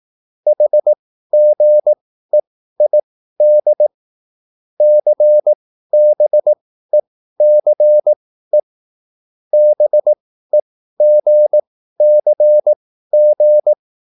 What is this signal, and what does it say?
Morse code, heard as a keyed tone, about 18 wpm, spaced as for 13 wpm, 600 Hz. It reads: HGEID CBECE BEGCG